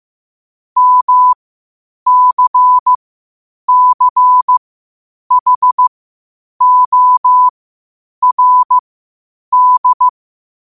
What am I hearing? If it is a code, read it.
Morse code, heard as a keyed tone, about 15 wpm, spaced as for 8 wpm, 1000 Hz. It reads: MCCHORD
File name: qotdfiles / 8wpm